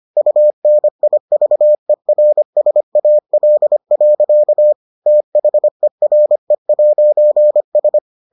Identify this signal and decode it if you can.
Morse code, heard as a keyed tone, about 25 wpm, 600 Hz. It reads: UNIVERSAL. THERE'S